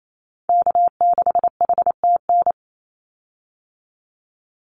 Morse code, heard as a keyed tone, about 28 wpm, 700 Hz. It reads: K6HTN
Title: Call Signs Repeated 3x - 28wpm